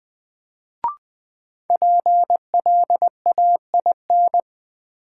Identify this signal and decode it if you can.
Morse code, heard as a keyed tone, about 20 wpm, 700 Hz. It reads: PLAIN
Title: Sentences from Top 400 Words - New Word Drill - 20wpm